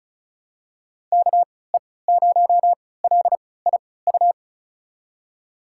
Morse code, heard as a keyed tone, about 36 wpm, spaced as for 20 wpm, 700 Hz. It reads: KE0LIU
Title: USA Call Signs - 3x Character Spacing - 35wpm